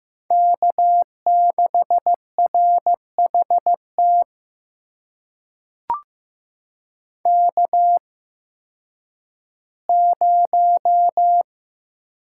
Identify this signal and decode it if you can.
Morse code, heard as a keyed tone, about 15 wpm, spaced as for 10 wpm, 700 Hz. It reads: K6RHT K0
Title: USA Call Signs - 8x Character Spacing - 15wpm